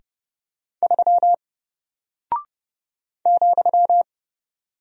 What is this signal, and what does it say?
Morse code, heard as a keyed tone, about 30 wpm, 700 Hz. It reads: 3 ,
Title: Single Character (Letter:Number:Punctuation:Prosign) Alt 30wpm